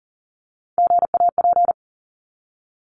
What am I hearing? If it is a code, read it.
Morse code, heard as a keyed tone, about 40 wpm, 700 Hz. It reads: GAP
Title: Top 1501-1750 Words 40wpm